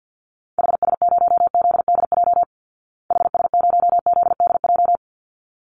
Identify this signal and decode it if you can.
Morse code, heard as a keyed tone, about 48 wpm, 700 Hz. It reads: HS0ZDJ HS0ZDJ
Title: Call Signs Repeated 3x - 50wpm